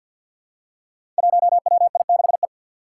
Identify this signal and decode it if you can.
Morse code, heard as a keyed tone, about 46 wpm, 700 Hz. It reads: 1WI6E
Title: Five Challenging Letter-Number Combinations 50wpm